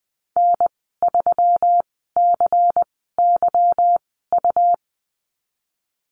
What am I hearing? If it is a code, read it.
Morse code, heard as a keyed tone, about 20 wpm, spaced as for 15 wpm, 700 Hz. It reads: N3CYU
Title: 147 Lesson 24g - Letter y - Call Signs 20wpm